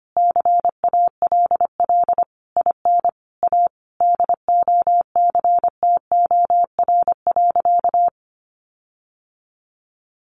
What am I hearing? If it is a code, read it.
Morse code, heard as a keyed tone, about 25 wpm, 700 Hz. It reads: CALL IN A DOCTOR.
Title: Sentences from Top 1500 Words 25wpm